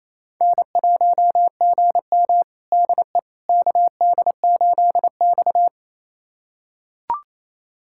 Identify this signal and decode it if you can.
Morse code, heard as a keyed tone, about 28 wpm, 700 Hz. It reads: N1GM DE KD8X
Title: USA Callsign DE Callsign - 4 Characters - 28wpm